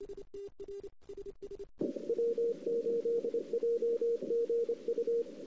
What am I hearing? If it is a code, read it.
Morse code, heard as a keyed tone, about 25 wpm, 470 Hz. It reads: W8JGU
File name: CW / 2008-ss-cw-k5tr-op-n5rz -01:00